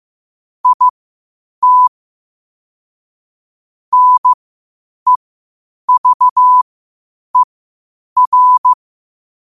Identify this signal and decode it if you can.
Morse code, heard as a keyed tone, about 15 wpm, spaced as for 8 wpm, 1000 Hz. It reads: IT NEVER
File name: qotdfiles / 8wpm